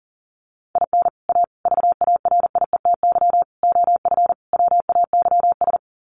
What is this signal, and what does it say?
Morse code, heard as a keyed tone, about 40 wpm, 700 Hz. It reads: IN A VARIETY OF WAYS
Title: Sets of 5 Words Encore 40wpm